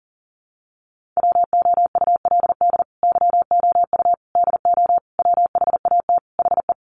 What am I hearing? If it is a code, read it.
Morse code, heard as a keyed tone, about 40 wpm, 700 Hz. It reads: WOULD YOU DO WHAT HE
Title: Sentences from Top 100 Words 40wpm